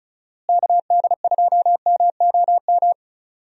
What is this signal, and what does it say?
Morse code, heard as a keyed tone, about 34 wpm, 700 Hz. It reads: KD2MOM
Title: USA Call Signs - 6 Characters - Repeated 3x - 35wpm